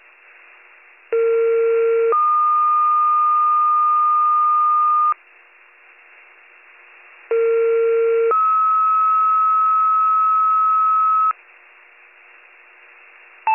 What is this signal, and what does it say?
Two-tone page, heard as a one-second tone, then a three-second tone, about 460 then 1220 Hz, about 460 then 1290 Hz. It is heard from 1.1 to 11.3 s.